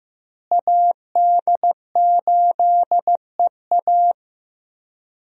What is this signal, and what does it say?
Morse code, heard as a keyed tone, about 15 wpm, 700 Hz. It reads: AD8EA
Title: USA Call Signs - 5 Characters - Repeated 3x - 15wpm